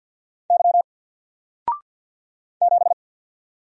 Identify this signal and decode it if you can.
Morse code, heard as a keyed tone, about 48 wpm, 700 Hz. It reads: Y 7
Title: Single Letter-Number Alt 50wpm